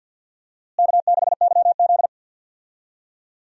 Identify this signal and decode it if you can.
Morse code, heard as a keyed tone, about 51 wpm, 700 Hz. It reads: K6YZ
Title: USA Call Signs - 4 Characters - Repeated 3x - 50wpm